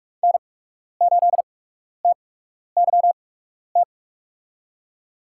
Morse code, heard as a keyed tone, about 47 wpm, spaced as for 12 wpm, 700 Hz. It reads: N8TYT
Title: USA Call Signs - 8x Character Spacing - 45wpm